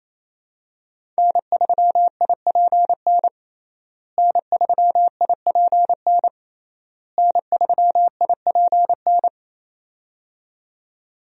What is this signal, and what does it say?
Morse code, heard as a keyed tone, about 27 wpm, 700 Hz. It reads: N3IPN N3IPN N3IPN
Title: USA Call Signs - 5 Characters - Repeated 3x - 28wpm